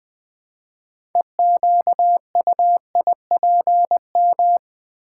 Morse code, heard as a keyed tone, about 20 wpm, 700 Hz. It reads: EQUIPM
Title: Build A Word - 3 Letters At A Time - 9-Letter Words - 20wpm